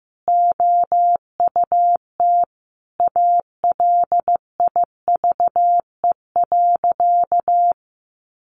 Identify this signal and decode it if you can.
Morse code, heard as a keyed tone, about 15 wpm, 700 Hz. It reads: OUT ALIVE.
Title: A Sound Of Thunder 15wpm